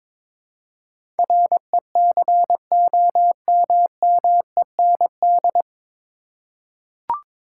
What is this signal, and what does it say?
Morse code, heard as a keyed tone, about 21 wpm, 700 Hz. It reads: RECOMMEND
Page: Morse Code Ninja practice 22wpm